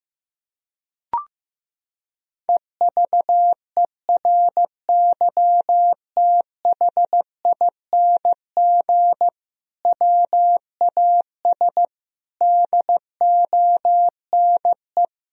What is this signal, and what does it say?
Morse code, heard as a keyed tone, about 16 wpm, 700 Hz. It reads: EVERYTHING WAS DONE
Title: Sentences from Top 1000 Words - Review Part 2 - 15wpm